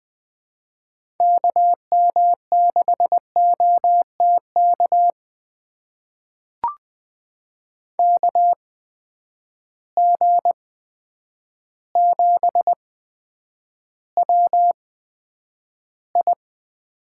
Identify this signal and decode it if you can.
Morse code, heard as a keyed tone, about 20 wpm, spaced as for 9 wpm, 700 Hz. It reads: KM6OTK KG7WI
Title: USA Call Signs - 6 Characters - 8x Character Spacing - 20wpm